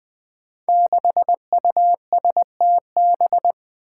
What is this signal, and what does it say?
Morse code, heard as a keyed tone, about 20 wpm, 700 Hz. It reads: 6USTB
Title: Five Challenging Letter-Number Combinations 20wpm